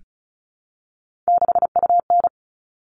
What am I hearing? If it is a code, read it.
Morse code, heard as a keyed tone, about 34 wpm, 700 Hz. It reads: 6UN
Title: All Cut Numbers - 3-Digit 35wpm